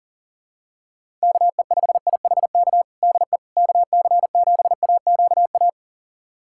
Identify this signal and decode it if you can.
Morse code, heard as a keyed tone, about 40 wpm, 700 Hz. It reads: KE5IHK DE KC7AQA
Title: USA Callsign DE Callsign - 6 Characters - 40wpm